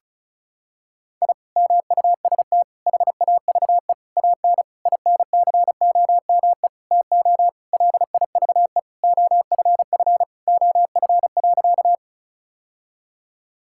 Morse code, heard as a keyed tone, about 34 wpm, 700 Hz. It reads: I MUST HAVE AN INCOME TO LIVE OFF OF.
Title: Sentences from Top 1000 Words - Review Part 5 - 35wpm